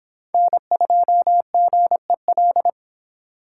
Morse code, heard as a keyed tone, about 25 wpm, 700 Hz. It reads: N2GEL